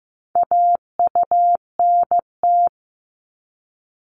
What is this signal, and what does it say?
Morse code, heard as a keyed tone, about 15 wpm, 700 Hz. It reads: AUNT
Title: Multi-letter Phonemes Repeated 3x 15wpm